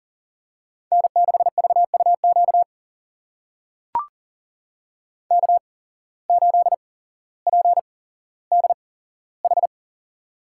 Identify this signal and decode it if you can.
Morse code, heard as a keyed tone, about 40 wpm, spaced as for 17 wpm, 700 Hz. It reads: N6VUQ K8PDH
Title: USA Call Signs - 5 Characters - 8x Character Spacing - 40wpm